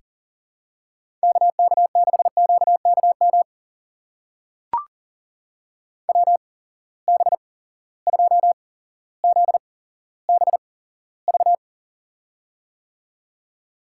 Morse code, heard as a keyed tone, about 37 wpm, spaced as for 17 wpm, 700 Hz. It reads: KK6QKM WB2ZBV
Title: USA Call Signs - 8x Character Spacing - 40wpm